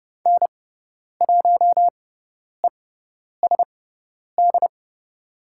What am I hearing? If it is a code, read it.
Morse code, heard as a keyed tone, about 30 wpm, spaced as for 10 wpm, 700 Hz. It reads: N1ESD
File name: Lesson 16g - Letter c - Call Signs 30wpm